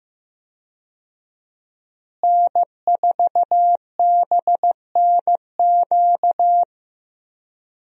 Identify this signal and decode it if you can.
Morse code, heard as a keyed tone, about 16 wpm, 700 Hz. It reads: N4BNQ
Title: USA Call Signs - 5 Characters - Repeated 2x - 15wpm